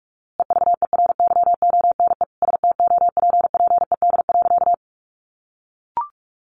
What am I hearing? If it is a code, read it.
Morse code, heard as a keyed tone, about 44 wpm, 700 Hz. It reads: EVERYONE STOPPED.
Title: A Sound Of Thunder 45wpm